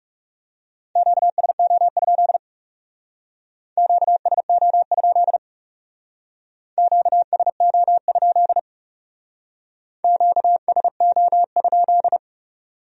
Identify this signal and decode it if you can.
Morse code, heard as a keyed tone, about 38 wpm, 700 Hz. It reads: QSO? QSO? QSO? QSO?